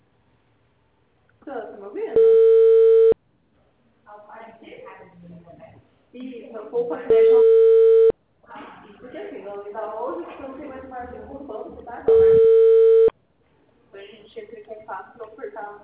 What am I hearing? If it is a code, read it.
Ringback tone (China, 450 Hz), with speech.